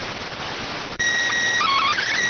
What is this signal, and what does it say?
SSTV (slow-scan television), PD 290.